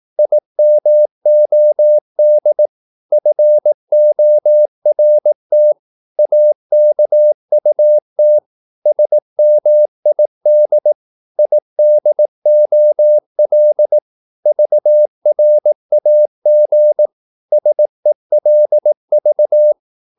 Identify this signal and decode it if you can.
Morse code, heard as a keyed tone, about 18 wpm, 600 Hz. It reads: IMOD FORT AKUT SMID IDOL VRAG SELV
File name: Korte ord DK 18wpm